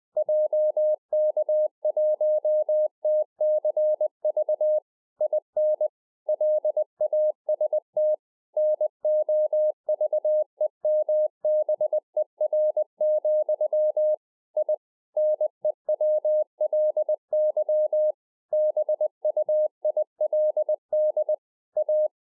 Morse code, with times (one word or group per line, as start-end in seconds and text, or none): 0.2-4.8 JK1TCV
5.2-5.9 IN
6.3-8.1 LAST
8.6-14.1 NOVEMBER,
14.6-14.7 I
15.2-18.1 NEWLY
18.5-21.3 BUILD
21.8-22.1 A